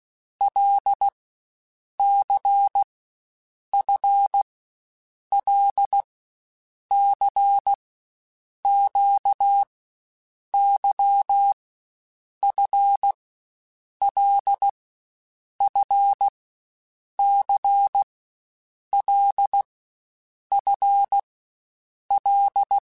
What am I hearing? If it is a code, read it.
Morse code, heard as a keyed tone, about 16 wpm, 800 Hz. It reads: L C F L C Q Y F L F C L F L
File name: Basic Letter Groups By7Groups / LFCYQ / LFCYQ@13